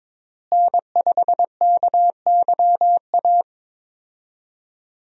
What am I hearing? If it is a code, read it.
Morse code, heard as a keyed tone, about 21 wpm, 700 Hz. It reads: N5KYA
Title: USA Call Signs - 5 Characters - Repeated 2x - 22wpm